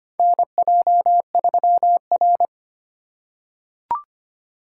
Morse code, heard as a keyed tone, about 25 wpm, 700 Hz. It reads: NJ3R
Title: USA Call Signs - 4 Characters - 4x Character Spacing - 25wpm